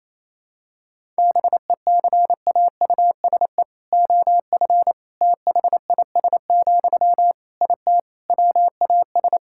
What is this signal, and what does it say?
Morse code, heard as a keyed tone, about 28 wpm, 700 Hz. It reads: BECAUSE OF THIS, IT WAS
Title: Sentences from Top 1000 Words - Review Part 6 - 28wpm